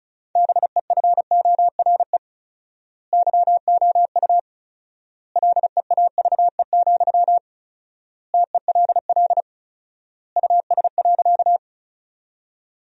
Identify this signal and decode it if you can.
Morse code, heard as a keyed tone, about 34 wpm, 700 Hz. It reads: BEFORE YOU LEAVE, TELL US.
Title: 4x Word-Spacing - Sentences from Top 400 Words - 35wpm